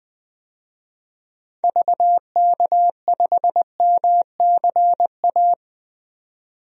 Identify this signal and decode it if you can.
Morse code, heard as a keyed tone, about 20 wpm, 700 Hz. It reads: VK5MCA
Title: DXCC-100 Call Signs - 6 Characters - 20wpm